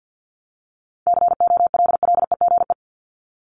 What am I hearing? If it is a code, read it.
Morse code, heard as a keyed tone, about 50 wpm, 700 Hz. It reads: COLLEGE
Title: Top 1001-1250 Words 50wpm